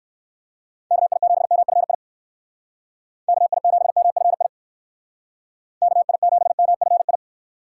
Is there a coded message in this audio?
Yes: Morse code from 0.9 to 7.2 s, heard as a keyed tone, about 64 wpm, 700 Hz. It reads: KI7MFI KI7MFI KI7MFI